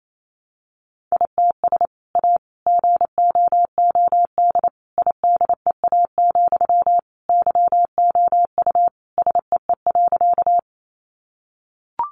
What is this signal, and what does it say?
Morse code, heard as a keyed tone, about 28 wpm, 700 Hz. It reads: ITS A GOOD IDEA, YOU SEE.
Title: Sentences from Top 600 Words 28wpm